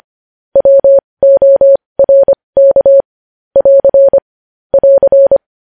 Morse code, heard as a keyed tone, about 25 wpm, spaced as for 20 wpm, 550 Hz. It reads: WORK <AR> <AR>
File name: podcast / News headlines-1x-25x20